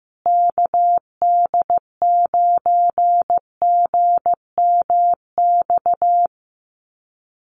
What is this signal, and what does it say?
Morse code, heard as a keyed tone, about 14 wpm, 700 Hz. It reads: KD9GMX
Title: USA Call Signs Repeated 3x 15wpm